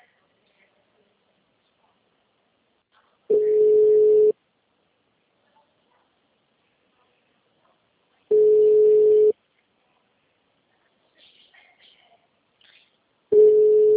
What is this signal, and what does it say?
Ringback tone (Europe, 425 Hz).
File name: out- -2009- - - .63139